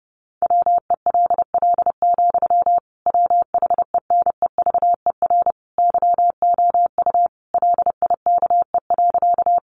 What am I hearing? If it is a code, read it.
Morse code, heard as a keyed tone, about 30 wpm, 700 Hz. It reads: WELL, WHENEVER YOU LIKE.
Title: Sentences from Top 1200 Words 30wpm